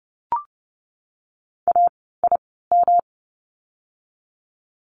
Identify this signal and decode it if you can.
Morse code, heard as a keyed tone, about 30 wpm, spaced as for 17 wpm, 700 Hz. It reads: AIM
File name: course / 30:10wpm / 108 Lesson 18f - Letter m - Words 30wpm